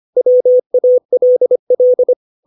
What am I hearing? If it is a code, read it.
Morse code, heard as a keyed tone, about 25 wpm, 500 Hz. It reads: WALL
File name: 632 words / 632 Words 25WPM